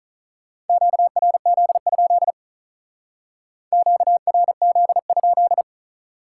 Morse code, heard as a keyed tone, about 45 wpm, 700 Hz. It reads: QRZ? QRZ?